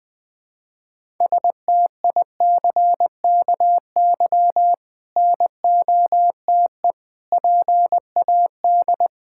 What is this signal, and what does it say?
Morse code, heard as a keyed tone, about 20 wpm, 700 Hz. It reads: STICKY NOTE PAD
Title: Difficult Words with Hints - 20wpm